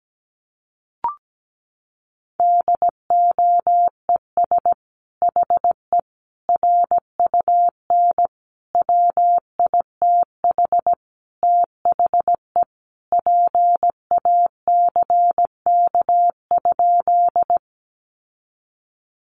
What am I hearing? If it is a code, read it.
Morse code, heard as a keyed tone, about 17 wpm, 700 Hz. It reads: DOES HE RUN WITH THE PACK?
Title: Sentences from Top 1300 Words 17wpm